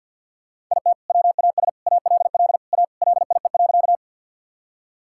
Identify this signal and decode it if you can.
Morse code, heard as a keyed tone, about 50 wpm, 700 Hz. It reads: IT WAS ALL A LIE.
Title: Sentences from Top 1600 Words 50wpm